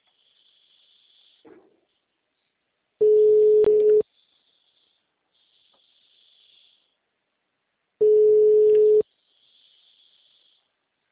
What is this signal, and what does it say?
Ringback tone (Europe, 425 Hz).